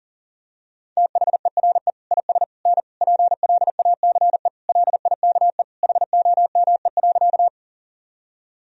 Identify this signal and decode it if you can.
Morse code, heard as a keyed tone, about 37 wpm, 700 Hz. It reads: THERE IS N PLACE LIKE HOME.
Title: Sentences from Top 200 Words and Abbreviations 40wpm